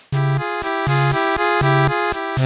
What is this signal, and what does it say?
There is no code.